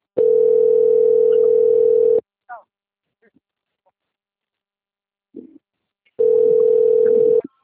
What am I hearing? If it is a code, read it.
Ringback tone (North America), 440 + 480 Hz.